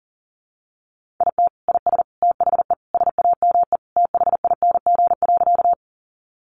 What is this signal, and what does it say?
Morse code, heard as a keyed tone, about 40 wpm, 700 Hz. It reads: IT IS THE SAME THING.
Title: Sentences from Top 200 Words 40wpm